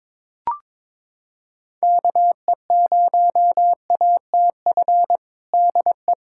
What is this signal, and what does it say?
Morse code, heard as a keyed tone, about 22 wpm, 700 Hz. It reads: KE0ATF DE
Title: USA Callsign DE Callsign 22wpm